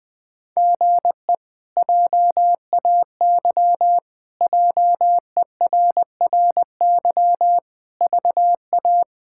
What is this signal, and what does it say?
Morse code, heard as a keyed tone, about 21 wpm, 700 Hz. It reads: GE JAY JERRY VA
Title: K1USN SST QSO Practice - 20wpm